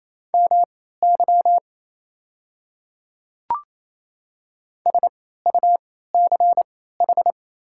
Morse code, heard as a keyed tone, about 27 wpm, spaced as for 16 wpm, 690 Hz. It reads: MY SUCH
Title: 3x Character Spacing - Top 200 Words - 28wpm